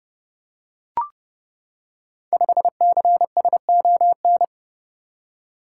Morse code, heard as a keyed tone, about 30 wpm, 700 Hz. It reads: 5CSON
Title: Code Groups - 5 Letter:Numbers - 30wpm